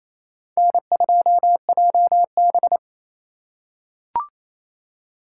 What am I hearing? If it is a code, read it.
Morse code, heard as a keyed tone, about 28 wpm, 700 Hz. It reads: N2JB
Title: USA Call Signs - 4 Characters - 4x Character Spacing - 28wpm